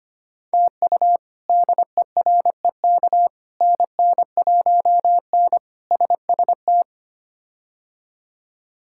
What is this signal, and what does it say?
Morse code, heard as a keyed tone, about 25 wpm, 700 Hz. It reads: TU DEREK NN1N SST
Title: K1USN SST QSO Practice - 25wpm